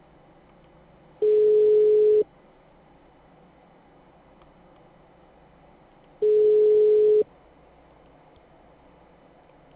Ringback tone (Europe, 425 Hz).
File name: out- -2001- - - .10786